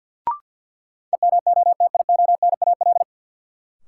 Morse code, heard as a keyed tone, about 46 wpm, 700 Hz. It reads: EMOTIONAL
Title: Sentences from Top 1000 Words - Review Part 5 - New Word Drill - 50wpm